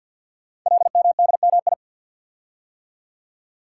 Morse code, heard as a keyed tone, about 51 wpm, 700 Hz. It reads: LMDMI
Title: Code Groups - 5 Letters - 50wpm